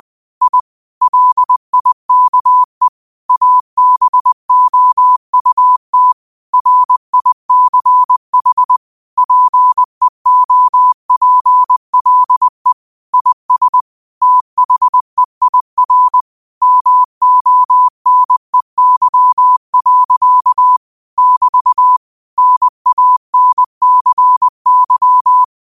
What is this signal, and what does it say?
Morse code, heard as a keyed tone, about 20 wpm, 1000 Hz. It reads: I LIKE ABOUT RICH PEOPLE IS THEIR MONEY. <BT> NANCY